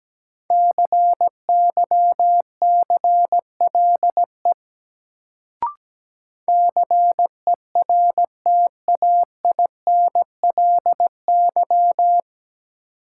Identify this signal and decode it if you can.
Morse code, heard as a keyed tone, about 17 wpm, 700 Hz. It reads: CYCLE CERTAINLY
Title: Sentences from Top 900 Words - New Word Drill - 17wpm